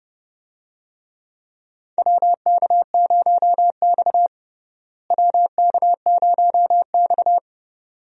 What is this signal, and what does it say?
Morse code, heard as a keyed tone, about 31 wpm, 700 Hz. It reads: WK0X WK0X
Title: USA Call Signs - 4 Characters - Repeated 2x - 30wpm